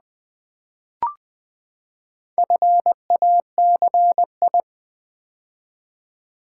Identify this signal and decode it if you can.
Morse code, heard as a keyed tone, about 20 wpm, 700 Hz. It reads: FACI
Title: Build A Word - 4 Letters At A Time - 10-Letter Words - 20wpm